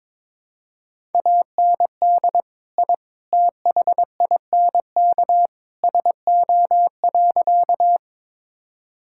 Morse code, heard as a keyed tone, about 22 wpm, 700 Hz. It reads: AND I THINK SO.
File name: practice / 22wpm / Sentences with 3-Word Phrases - 22wpm